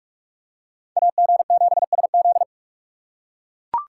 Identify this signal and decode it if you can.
Morse code, heard as a keyed tone, about 42 wpm, 700 Hz. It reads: AG7SZ
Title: USA Call Signs - 6x Character Spacing - 45wpm